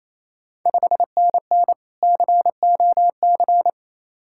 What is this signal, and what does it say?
Morse code, heard as a keyed tone, about 28 wpm, 690 Hz. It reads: 5NN COC
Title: Ontario QSO Party Exchange 28wpm